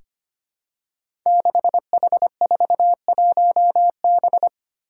Morse code, heard as a keyed tone, about 25 wpm, 700 Hz. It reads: 6H41B
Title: Five Challenging Letter-Number Combinations 25wpm